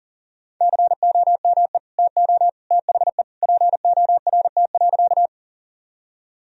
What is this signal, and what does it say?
Morse code, heard as a keyed tone, about 37 wpm, 700 Hz. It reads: COME TO THE PORT.
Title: Sentences from Top 1000 Words - Review Part 1 - 40wpm